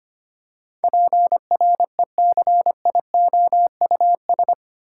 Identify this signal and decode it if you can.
Morse code, heard as a keyed tone, about 25 wpm, 700 Hz. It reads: PRECIOUS
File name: 8 Letter Words - First 3 Letters Given - 25wpm